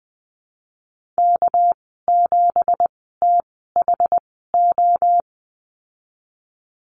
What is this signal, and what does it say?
Morse code, heard as a keyed tone, about 19 wpm, spaced as for 14 wpm, 700 Hz. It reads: K7THO
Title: 245 Lesson 39g - Letter z - Call Signs 20wpm